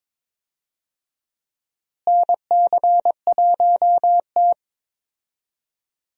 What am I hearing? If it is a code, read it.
Morse code, heard as a keyed tone, about 22 wpm, 700 Hz. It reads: NC1T
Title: USA Call Signs - 4 Characters - Repeated 3x - 22wpm